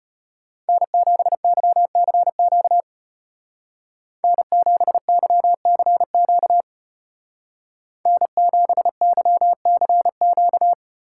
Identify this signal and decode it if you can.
Morse code, heard as a keyed tone, about 30 wpm, 700 Hz. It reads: N7YCQ N7YCQ N7YCQ
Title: Speed Racing - USA 1x3 Call Signs 25wpm